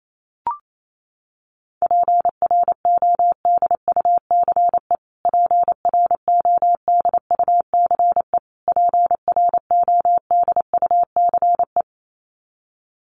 Morse code, heard as a keyed tone, about 28 wpm, 700 Hz. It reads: PRODUCE PRODUCE PRODUCE
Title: Top 500 Words Repeated 28wpm